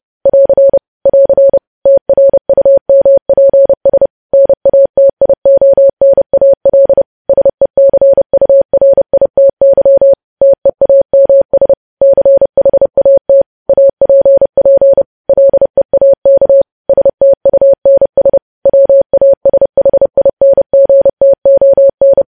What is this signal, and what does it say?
Morse code, heard as a keyed tone, about 30 wpm, 550 Hz. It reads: <AR> <AR> TRUMPS NATIONAL SECURITY TEAMS CHAT APP LEAK STUNS WASHINGTON
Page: news headlines in morse code at 30 wpm